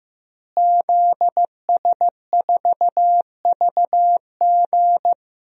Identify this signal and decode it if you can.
Morse code, heard as a keyed tone, about 15 wpm, 700 Hz. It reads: ZS4VG
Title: DXCC-100 Call Signs - 5 Characters - Repeated 3x - 15wpm